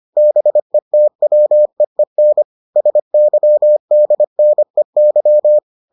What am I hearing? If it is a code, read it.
Morse code, heard as a keyed tone, about 25 wpm, 600 Hz. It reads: BETWEEN SYDNEY